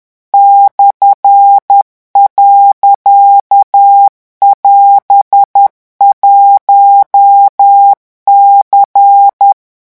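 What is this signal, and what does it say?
Morse code, heard as a keyed tone, about 10 wpm, 800 Hz. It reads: /.<AS>1C
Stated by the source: French Morse code revision lesson 18 at 10 words per minute.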